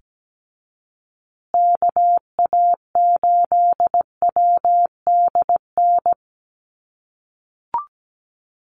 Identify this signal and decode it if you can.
Morse code, heard as a keyed tone, about 17 wpm, 700 Hz. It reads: KA8WDN
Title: USA Call Signs Repeated 3x 17wpm